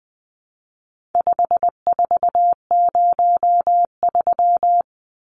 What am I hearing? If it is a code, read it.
Morse code, heard as a keyed tone, about 20 wpm, 700 Hz. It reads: 5403
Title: Numbers - 4-Digit 20wpm